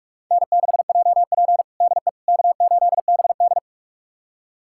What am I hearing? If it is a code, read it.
Morse code, heard as a keyed tone, about 46 wpm, 700 Hz. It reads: N6JP DE K8BD
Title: USA Callsign DE Callsign - 4 Characters - 45wpm